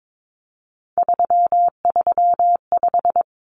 Morse code, heard as a keyed tone, about 22 wpm, 700 Hz. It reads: 335
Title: Numbers - 3-Digit 22wpm